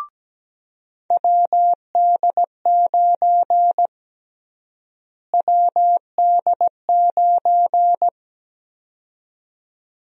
Morse code, heard as a keyed tone, about 17 wpm, 700 Hz. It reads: WD9 WD9
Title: USA Call Sign Prefixes - Repeated 2x - 17wpm